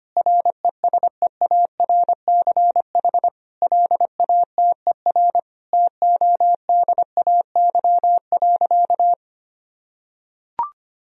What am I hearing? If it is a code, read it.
Morse code, heard as a keyed tone, about 25 wpm, 700 Hz. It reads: RESEARCH LATER TODAY.
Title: Sentences from Top 1000 Words - Review Part 4 - 25wpm